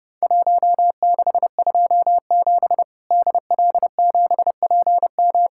Morse code, heard as a keyed tone, about 30 wpm, 700 Hz. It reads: 1627 DL7PM
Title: Worked All Europe DX Contest - QTC Exchange - 30wpm